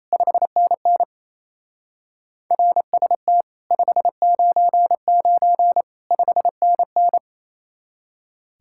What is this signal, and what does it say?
Morse code, heard as a keyed tone, about 34 wpm, 700 Hz. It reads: 5NN RST 599 5NN
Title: Speed-Racing Common QSO Phrases Encore - 28wpm